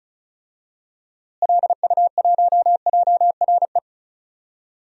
Morse code, heard as a keyed tone, about 36 wpm, 700 Hz. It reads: LU1JRE
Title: DXCC-100 Call Signs - 6 Characters - Repeated 3x - 35wpm